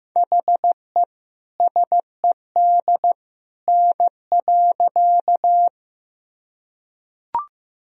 Morse code, heard as a keyed tone, about 15 wpm, 700 Hz. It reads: HE SED N.
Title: Sentences from Top 100 Words and Abbreviations 15wpm